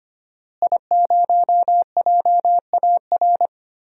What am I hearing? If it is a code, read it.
Morse code, heard as a keyed tone, about 25 wpm, 700 Hz. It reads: I0JAR
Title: DXCC-100 Call Signs - 5 Characters - Repeated 3x - 25wpm